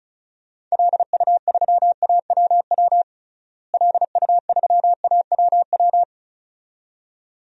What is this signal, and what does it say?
Morse code, heard as a keyed tone, about 36 wpm, 700 Hz. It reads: LU3AWW LU3AWW
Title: DXCC-100 Call Signs - 6 Characters - Repeated 2x - 35wpm